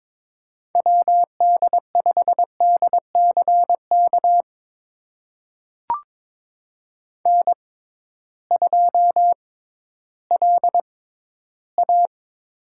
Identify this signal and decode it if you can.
Morse code, heard as a keyed tone, about 22 wpm, spaced as for 13 wpm, 700 Hz. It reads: WD5DCK N2LA
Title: USA Call Signs - 6x Character Spacing - 22wpm